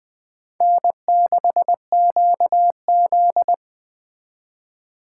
Morse code, heard as a keyed tone, about 20 wpm, 700 Hz. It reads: N6QZ
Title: Speed Racing - USA 1x2 Call Signs 17wpm